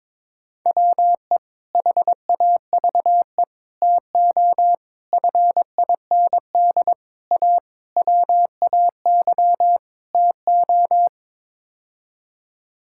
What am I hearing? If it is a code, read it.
Morse code, heard as a keyed tone, about 22 wpm, 700 Hz. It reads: WE HAVE TO FIND A WAY TO
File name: practice / 22wpm / Sets of 7 Words - 22wpm